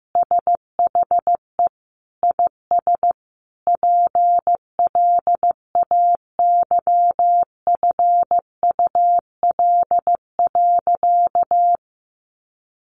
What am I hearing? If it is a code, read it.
Morse code, heard as a keyed tone, about 15 wpm, 700 Hz. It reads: SHE IS PLAYFUL.
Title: Easy Phrases 15wpm